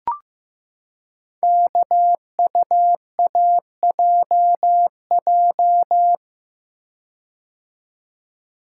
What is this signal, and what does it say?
Morse code, heard as a keyed tone, about 15 wpm, 700 Hz. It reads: KUAJJ
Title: Code Groups - 5 Letters - 15wpm